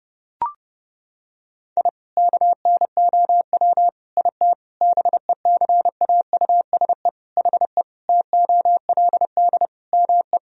Morse code, heard as a keyed tone, about 31 wpm, 700 Hz. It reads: I KNOW IT BECAUSE HE TOLD ME
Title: Sentences from Top 1000 Words - Review Part 3 - 30wpm